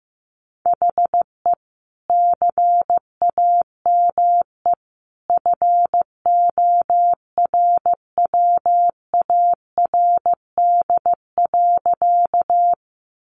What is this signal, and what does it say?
Morse code, heard as a keyed tone, about 15 wpm, 700 Hz. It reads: HE CAME FORWARD.
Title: Sentences from Top 1300 Words 15wpm